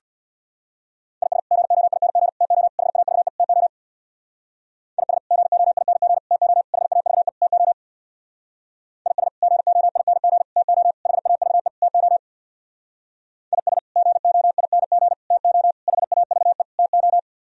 Morse code, heard as a keyed tone, about 61 wpm, 700 Hz. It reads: IS GOING TO HAVE TO IS GOING TO HAVE TO IS GOING TO HAVE TO IS GOING TO HAVE TO
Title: Speed-Racing - Sets of 5 Words 50wpm